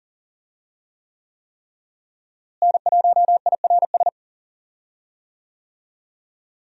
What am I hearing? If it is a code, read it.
Morse code, heard as a keyed tone, about 40 wpm, 700 Hz. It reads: N1IRS
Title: USA Call Signs - 5 Characters - Repeated 3x - 40wpm